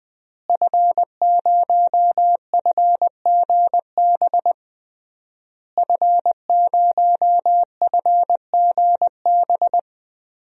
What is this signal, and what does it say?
Morse code, heard as a keyed tone, about 20 wpm, 700 Hz. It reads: F0FGB F0FGB
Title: DXCC-100 Call Signs - 5 Characters - Repeated 3x - 20wpm